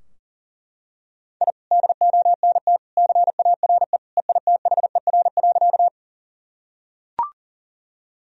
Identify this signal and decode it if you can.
Morse code, heard as a keyed tone, about 40 wpm, 690 Hz. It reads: I DONT CARE EITHER.
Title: Sentences from Top 1000 Words - Review Part 4 - 40wpm